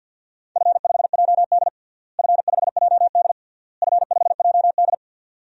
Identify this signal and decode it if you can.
Morse code, heard as a keyed tone, about 53 wpm, 700 Hz. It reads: F5JD F5JD F5JD
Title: DXCC-100 Call Signs - 4 Characters - Repeated 3x - 50wpm